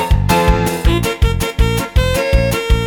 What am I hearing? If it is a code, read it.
There is no code.